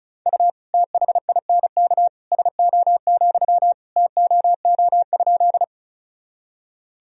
Morse code, heard as a keyed tone, about 34 wpm, 690 Hz. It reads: U THINK SO, TOO?
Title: Sentences from Top 200 Words and Abbreviations 35wpm